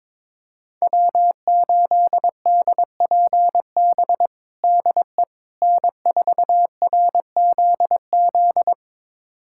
Morse code, heard as a keyed tone, about 22 wpm, 700 Hz. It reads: W8DPB DE N4RZZ
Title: USA Callsign DE Callsign - 5 Characters - 22wpm